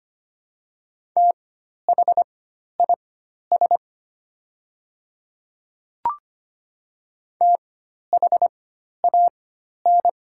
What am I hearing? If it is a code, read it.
Morse code, heard as a keyed tone, about 25 wpm, spaced as for 12 wpm, 700 Hz. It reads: THIS THAN
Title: 4x Character Spacing - Top 100 Words - 25wpm